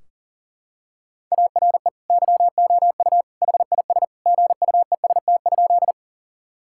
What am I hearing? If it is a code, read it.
Morse code, heard as a keyed tone, about 40 wpm, 700 Hz. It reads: ARE YOU HIS GUEST?
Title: Sentences from Top 1600 Words 40wpm